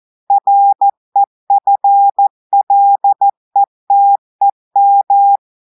Morse code, heard as a keyed tone, about 14 wpm, 800 Hz. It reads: REFLETEM
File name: audios / portugais / portugais texte en clair 1 vitesse 14 mots minutes